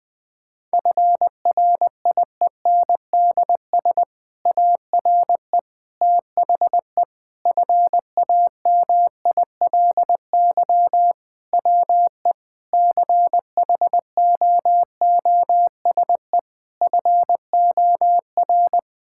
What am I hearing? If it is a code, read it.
Morse code, heard as a keyed tone, about 20 wpm, 700 Hz. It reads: FRIENDS ARE THE FAMILY WE CHOOSE FOR
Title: Difficult Sentences with Hints - 20wpm